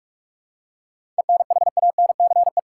Morse code, heard as a keyed tone, about 46 wpm, 700 Hz. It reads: ENHANCE
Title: Seven Letter Words - 45wpm